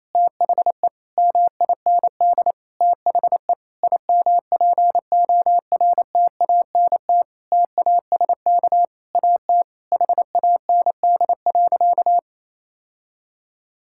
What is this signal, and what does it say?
Morse code, heard as a keyed tone, about 28 wpm, 700 Hz. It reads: THE MIND THE IMPORTANT TASK AT HAND.
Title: Sentences from Top 1600 Words 28wpm